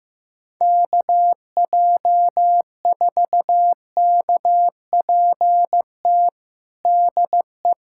Morse code, heard as a keyed tone, about 15 wpm, 700 Hz. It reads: KJ4KPT DE
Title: USA Callsign DE Callsign - 6 Characters - 15wpm